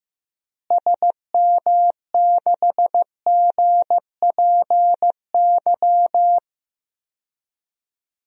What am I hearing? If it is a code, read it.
Morse code, heard as a keyed tone, about 15 wpm, 700 Hz. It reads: SM6GPY
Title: DXCC-100 Call Signs - 6 Characters - Repeated 3x - 15wpm